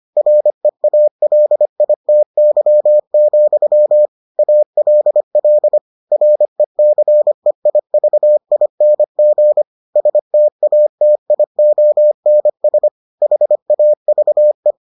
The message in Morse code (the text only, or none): REALITY, ALL RECEIVING STATIONS HAVE